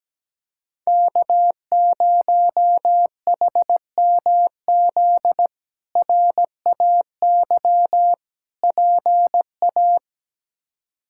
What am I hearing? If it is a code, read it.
Morse code, heard as a keyed tone, about 17 wpm, 700 Hz. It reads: K0HMZ RAY PA
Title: K1USN SST QSO Practice - 17wpm